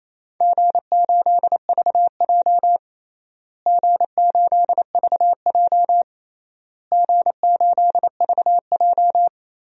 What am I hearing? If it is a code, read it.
Morse code, heard as a keyed tone, about 28 wpm, 700 Hz. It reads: G8VJ G8VJ G8VJ
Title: DXCC-100 Call Signs - 4 Characters - Repeated 3x - 28wpm